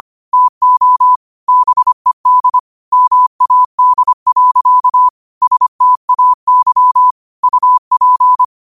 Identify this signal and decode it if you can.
Morse code, heard as a keyed tone, about 25 wpm, 1000 Hz. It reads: TO BED MAD. STAY UP
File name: qotdfiles / 25wpm